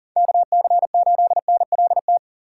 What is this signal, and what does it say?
Morse code, heard as a keyed tone, about 40 wpm, 700 Hz. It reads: KC8NLT
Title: USA Call Signs - 6 Characters - 3x Character Spacing - 40wpm